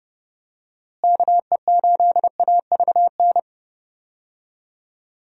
Morse code, heard as a keyed tone, about 31 wpm, 700 Hz. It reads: KE8AVN
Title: USA Call Signs - 6 Characters - Repeated 3x - 30wpm